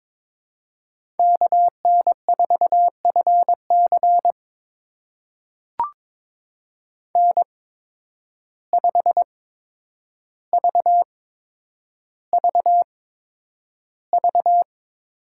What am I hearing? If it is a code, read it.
Morse code, heard as a keyed tone, about 22 wpm, spaced as for 9 wpm, 700 Hz. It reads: KN4FC N5VVV
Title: USA Call Signs - 5 Characters - 8x Character Spacing - 22wpm